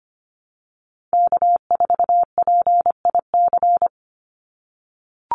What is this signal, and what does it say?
Morse code, heard as a keyed tone, about 25 wpm, 700 Hz. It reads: K4PIC
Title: NA SOTA Chaser Callsigns 25wpm